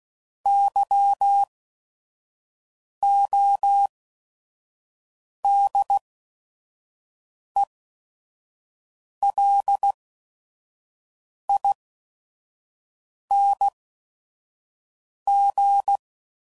Morse code, heard as a keyed tone, about 16 wpm, spaced as for 5 wpm, 800 Hz. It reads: YODELING
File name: Suffixes / Special ING's / VerySlowToFaster ING's